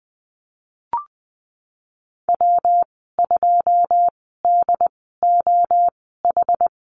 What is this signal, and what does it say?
Morse code, heard as a keyed tone, about 20 wpm, spaced as for 15 wpm, 700 Hz. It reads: W2DOH
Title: 140 Lesson 23g - Letter f - Call Signs 20wpm